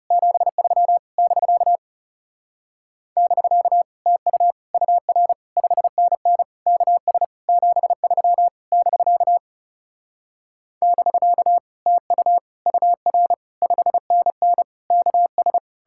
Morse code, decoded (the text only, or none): 73 <BK> <BK> TU UR 5NN KS 73 <BK> <BK> TU UR 5NN KS